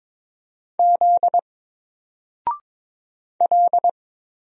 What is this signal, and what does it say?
Morse code, heard as a keyed tone, about 22 wpm, 700 Hz. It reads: Z L
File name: Single-Letter-Number-22wpm